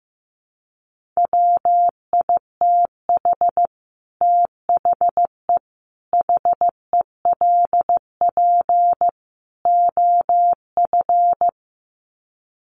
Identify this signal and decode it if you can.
Morse code, heard as a keyed tone, about 15 wpm, 700 Hz. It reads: WITH THE HELP OF
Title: Sets of 4 Words 15wpm